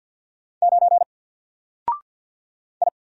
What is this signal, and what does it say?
Morse code, heard as a keyed tone, about 49 wpm, 700 Hz. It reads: 9 I
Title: Single Character (Letter:Number:Punctuation:Prosign) Alt 50wpm